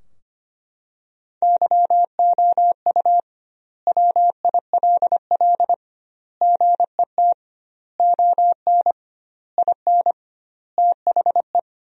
Morse code, heard as a keyed tone, about 25 wpm, 700 Hz. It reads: YOU WILL GET ON IN THE
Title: 2x Word-Spacing - Sentences from Top 200 Words - 25wpm